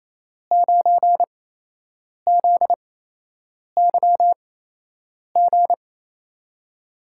Morse code, heard as a keyed tone, about 27 wpm, spaced as for 8 wpm, 700 Hz. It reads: 9ZYG
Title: USA Call Signs - 5 Characters - 8x Character Spacing - 28wpm